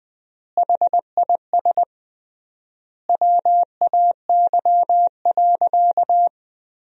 Morse code, heard as a keyed tone, about 20 wpm, 700 Hz. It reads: HIS WAY.
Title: 3x Word-Spacing - Sentences from Top 200 Words - 20wpm